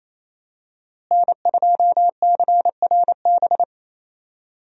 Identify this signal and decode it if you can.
Morse code, heard as a keyed tone, about 27 wpm, 700 Hz. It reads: N2CRB
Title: USA Call Signs - 5 Characters - Repeated 2x - 28wpm